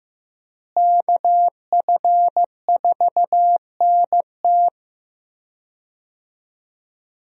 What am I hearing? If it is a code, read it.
Morse code, heard as a keyed tone, about 15 wpm, 700 Hz. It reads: KF4NT
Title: USA Call Signs - 5 Characters - Repeated 3x - 15wpm